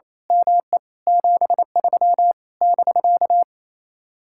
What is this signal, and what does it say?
Morse code, heard as a keyed tone, about 26 wpm, 700 Hz. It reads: ME 73 <BK>
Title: Speed-Racing Common QSO Phrases - 28wpm